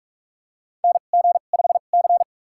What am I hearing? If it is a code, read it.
Morse code, heard as a keyed tone, about 42 wpm, spaced as for 31 wpm, 700 Hz. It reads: NG5C
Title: USA Call Signs - 4 Characters - 2x Character Spacing - 45wpm